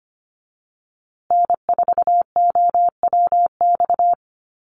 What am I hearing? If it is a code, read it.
Morse code, heard as a keyed tone, about 25 wpm, 700 Hz. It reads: N4OWX
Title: USA Call Signs Repeated 3x 25wpm